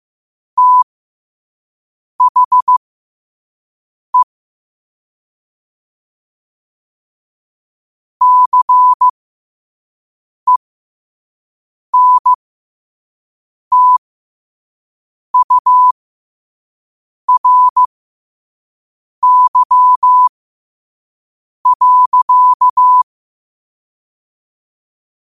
Morse code, heard as a keyed tone, about 15 wpm, spaced as for 5 wpm, 1000 Hz. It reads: THE CENTURY.
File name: qotdfiles / 5wpm